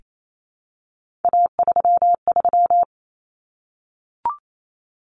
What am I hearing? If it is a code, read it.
Morse code, heard as a keyed tone, about 28 wpm, 700 Hz. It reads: A33
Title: Common Cut Numbers - 3-Digit 28wpm